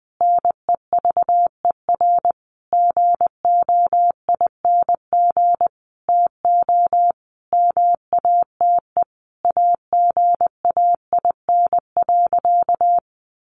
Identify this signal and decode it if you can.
Morse code, heard as a keyed tone, about 20 wpm, 700 Hz. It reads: NEVER GOING TO MATE AGAIN.
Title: A Sound Of Thunder 20wpm